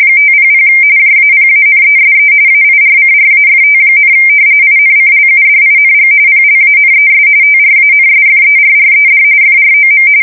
RTTY, about 45.45 baud. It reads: 500 JAPANESE AIRCRAFT, INCLUDING PLANES TRANSFERRED FROM
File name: fsk170 / battleshipnj